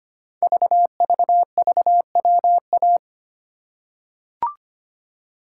Morse code, heard as a keyed tone, about 25 wpm, 700 Hz. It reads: VVVWA
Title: Code Groups - 5 Letters - 25wpm